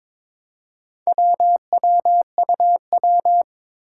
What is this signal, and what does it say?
Morse code, heard as a keyed tone, about 22 wpm, 700 Hz. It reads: WWUW
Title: Four Challenging Letter-Number Combinations 22wpm